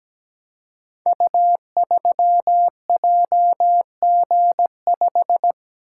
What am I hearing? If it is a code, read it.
Morse code, heard as a keyed tone, about 17 wpm, 700 Hz. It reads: U3JG5
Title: Code Groups - 5 Letter:Numbers - 17wpm